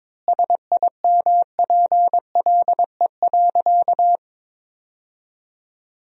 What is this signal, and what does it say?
Morse code, heard as a keyed tone, about 22 wpm, 700 Hz. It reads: SIMPLE.
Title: 3x Word-Spacing - Sentences from Top 400 Words - 22wpm